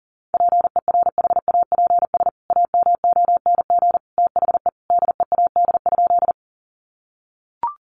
Morse code, heard as a keyed tone, about 37 wpm, 700 Hz. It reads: PERHAPS AMONG THE DEAD?
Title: Sentences from Top 1500 Words 40wpm